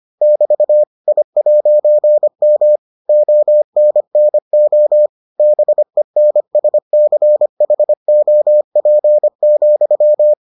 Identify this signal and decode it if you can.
Morse code, heard as a keyed tone, about 25 wpm, 600 Hz. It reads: <BT> I'M ONNO BENSCHOP,